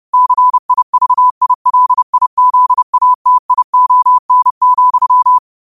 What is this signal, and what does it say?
Morse code, heard as a keyed tone, about 30 wpm, 1000 Hz. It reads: CIVILIZATION,
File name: qotdfiles / 30wpm